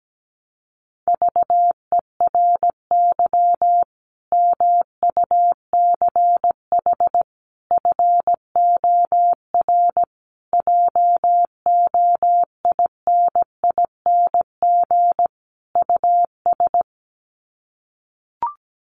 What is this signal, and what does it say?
Morse code, heard as a keyed tone, about 17 wpm, 700 Hz. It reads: VERY MUCH FOR JOINING US
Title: Sets of 5 Words 17wpm